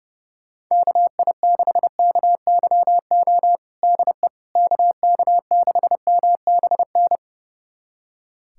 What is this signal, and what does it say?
Morse code, heard as a keyed tone, about 32 wpm, 700 Hz. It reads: KI6KYO DE KK6MBN
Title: USA Callsign DE Callsign - 6 Characters - 30wpm